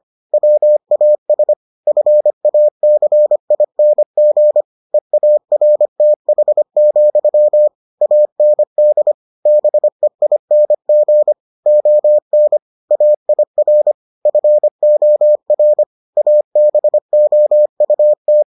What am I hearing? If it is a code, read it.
Morse code, heard as a keyed tone, about 25 wpm, 600 Hz. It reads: WAS FACING EARTH, AND BEING ON AIR FOR ABOUT